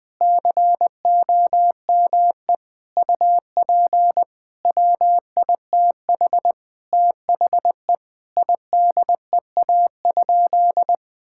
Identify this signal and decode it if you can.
Morse code, heard as a keyed tone, about 20 wpm, 700 Hz. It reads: COME UP WITH THE IDEA?
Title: Sentences with 8-Word Phrases - 20wpm